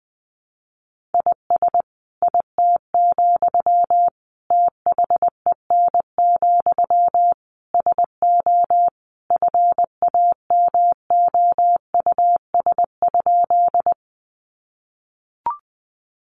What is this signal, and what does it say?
Morse code, heard as a keyed tone, about 20 wpm, 700 Hz. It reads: IS IT, THEN, SO FAMOUS?
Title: Sentences from Top 1500 Words 20wpm